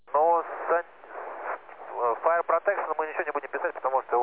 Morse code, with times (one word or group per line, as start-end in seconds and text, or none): none